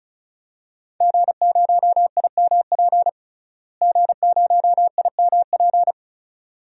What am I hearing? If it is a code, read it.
Morse code, heard as a keyed tone, about 36 wpm, 700 Hz. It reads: G0IMP G0IMP
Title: DXCC-100 Call Signs - 5 Characters - Repeated 2x - 35wpm